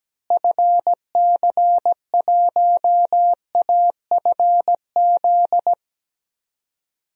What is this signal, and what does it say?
Morse code, heard as a keyed tone, about 17 wpm, 700 Hz. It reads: FC1AFZ
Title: DXCC-100 Call Signs - 6 Characters - Repeated 2x - 17wpm